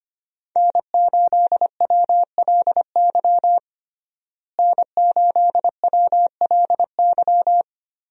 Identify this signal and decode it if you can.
Morse code, heard as a keyed tone, about 25 wpm, 700 Hz. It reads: N8WLY N8WLY
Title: USA Call Signs - 5 Characters - Repeated 3x - 25wpm